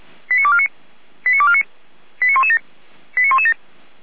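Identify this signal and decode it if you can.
Five-tone selective call (ZVEI), heard as quick runs of five tones, tones of about 70 ms. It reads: 89249 89249 89117 89117
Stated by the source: ZVEI five-tone selective call received on an SDR.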